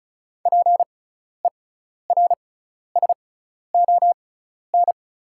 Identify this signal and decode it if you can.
Morse code, heard as a keyed tone, about 34 wpm, spaced as for 12 wpm, 700 Hz. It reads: PERSON
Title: 6x Character Spacing - Top 500 Words - 35wpm